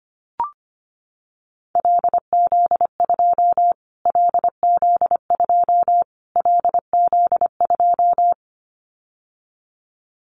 Morse code, heard as a keyed tone, about 25 wpm, 700 Hz. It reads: LZ2 LZ2 LZ2
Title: Callsign Prefixes for Top 25 DXCC - Repeated 3x - 25wpm